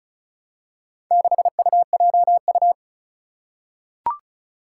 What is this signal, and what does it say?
Morse code, heard as a keyed tone, about 36 wpm, 700 Hz. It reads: 6UJU
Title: Four Challenging Letter-Number Combinations 35wpm